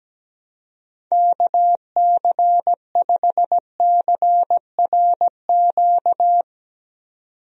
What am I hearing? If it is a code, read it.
Morse code, heard as a keyed tone, about 17 wpm, 700 Hz. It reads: KC5CRQ
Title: USA Call Signs - 6 Characters - 17wpm